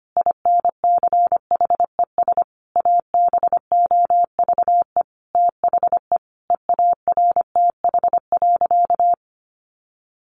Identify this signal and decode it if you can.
Morse code, heard as a keyed tone, about 25 wpm, 700 Hz. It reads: INCHES ABOVE THE EARTH.
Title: A Sound Of Thunder 25wpm